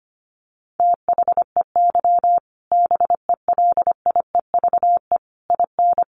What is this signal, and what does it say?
Morse code, heard as a keyed tone, about 25 wpm, 700 Hz. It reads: THEY BELIEVE IN
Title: Sentences from Top 1000 Words 25wpm